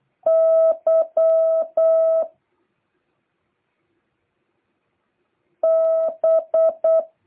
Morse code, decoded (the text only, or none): Y B